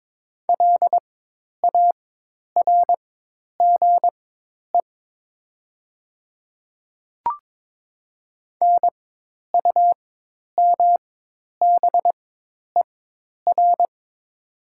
Morse code, heard as a keyed tone, about 22 wpm, spaced as for 10 wpm, 700 Hz. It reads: LARGE NUMBER
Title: 4x Character Spacing - Top 300 Words - 22wpm